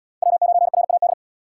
Morse code, heard as a keyed tone, about 76 wpm, 700 Hz. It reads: W9RAD